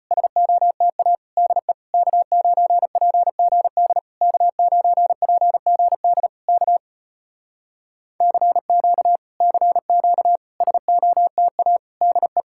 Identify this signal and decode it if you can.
Morse code, heard as a keyed tone, about 34 wpm, 700 Hz. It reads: SOTA DE K9PGD K9PGD K CQ CQ SOTA DE